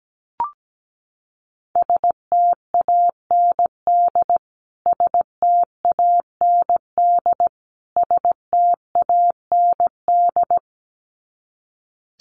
Morse code, heard as a keyed tone, about 17 wpm, 700 Hz. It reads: STAND STAND STAND
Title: Top 300 Words Repeated 17wpm